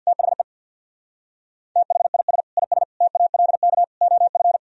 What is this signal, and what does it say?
Morse code, heard as a keyed tone, about 62 wpm, 700 Hz. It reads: THE THIS IS TALK OF
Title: Speed-Racing - Sets of 5 Words 50wpm